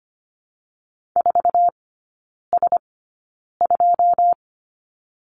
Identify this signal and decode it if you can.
Morse code, heard as a keyed tone, about 25 wpm, spaced as for 9 wpm, 700 Hz. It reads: 4S2
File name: course / 25:10wpm / 093 Lesson 16e - Letter c - Three Characters 25wpm